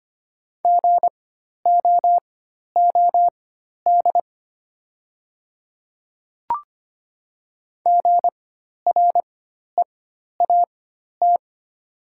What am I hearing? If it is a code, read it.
Morse code, heard as a keyed tone, about 25 wpm, spaced as for 12 wpm, 700 Hz. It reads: GOOD GREAT
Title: 4x Character Spacing - Top 300 Words - 25wpm